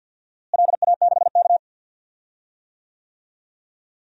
Morse code, heard as a keyed tone, about 51 wpm, 700 Hz. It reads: LA6K
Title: DXCC-100 Call Signs - 4 Characters - Repeated 2x - 50wpm